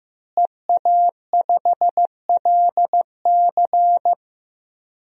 Morse code, heard as a keyed tone, about 15 wpm, 700 Hz. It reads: EA5LC
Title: DXCC-100 Call Signs - 5 Characters - Repeated 2x - 15wpm